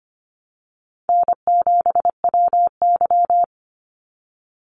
Morse code, heard as a keyed tone, about 24 wpm, 700 Hz. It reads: N7WY
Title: Easy Call Signs Repeated 3x - 25wpm